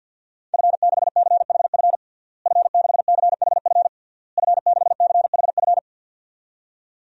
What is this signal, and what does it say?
Morse code, heard as a keyed tone, about 49 wpm, 700 Hz. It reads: F6CHF F6CHF F6CHF
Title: DXCC-100 Call Signs - 5 Characters - Repeated 3x - 50wpm